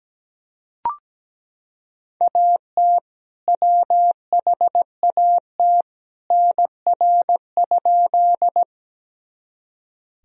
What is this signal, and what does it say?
Morse code, heard as a keyed tone, about 17 wpm, 700 Hz. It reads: AT WHAT NR?
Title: Sentences from Top 100 Words and Abbreviations 17wpm